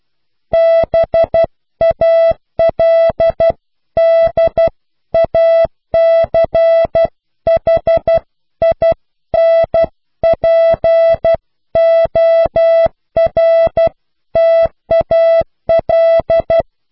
Morse code, heard as a keyed tone, about 12 wpm, 660 Hz. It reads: BALDACHINPORTAL
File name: cw / Pseudo woorden (15)